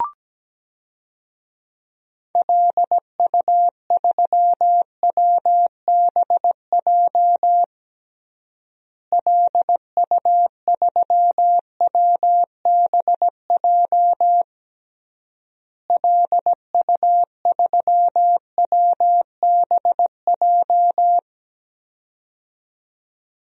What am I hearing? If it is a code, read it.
Morse code, heard as a keyed tone, about 17 wpm, 700 Hz. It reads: LU3WBJ LU3WBJ LU3WBJ
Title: DXCC-100 Call Signs - 6 Characters - Repeated 3x - 17wpm